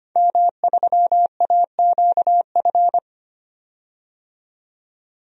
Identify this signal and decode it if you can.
Morse code, heard as a keyed tone, about 25 wpm, 700 Hz. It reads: M3AQF
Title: DXCC-100 Call Signs - 5 Characters - 25wpm